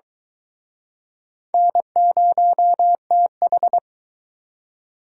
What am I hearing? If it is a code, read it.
Morse code, heard as a keyed tone, about 24 wpm, 700 Hz. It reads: N0TH